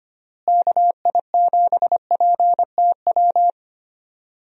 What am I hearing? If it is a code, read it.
Morse code, heard as a keyed tone, about 25 wpm, 700 Hz. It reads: KI7PTW
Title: USA Call Signs - 6 Characters - Repeated 2x - 25wpm